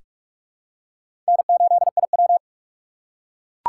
Morse code, heard as a keyed tone, about 44 wpm, 700 Hz. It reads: N8IW
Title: USA Call Signs - 4 Characters - 4x Character Spacing - 45wpm